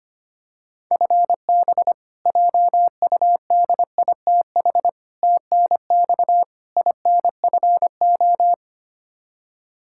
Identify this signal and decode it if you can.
Morse code, heard as a keyed tone, about 25 wpm, 700 Hz. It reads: FB JUDITH TNX INFO
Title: K1USN SST QSO Practice 25wpm